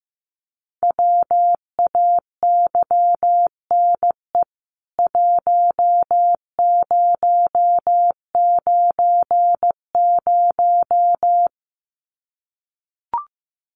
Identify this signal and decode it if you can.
Morse code, heard as a keyed tone, about 15 wpm, 700 Hz. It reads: WAYNE 1090
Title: CWT Exchange 15wpm